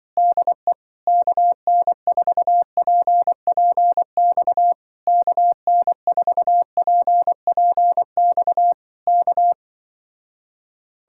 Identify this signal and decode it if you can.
Morse code, heard as a keyed tone, about 24 wpm, 700 Hz. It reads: DE KN4PPX KN4PPX K
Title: Speed-Racing Common QSO Phrases Encore - 20wpm